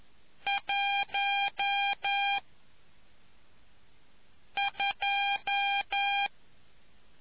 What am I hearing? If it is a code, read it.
Morse code, heard as a keyed tone, about 11 wpm, 800 Hz. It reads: 1 2